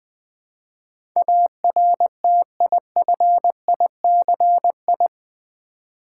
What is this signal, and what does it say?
Morse code, heard as a keyed tone, about 20 wpm, 700 Hz. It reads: ARTIFICI
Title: Build A Word - 4 Letters At A Time - 10-Letter Words - 20wpm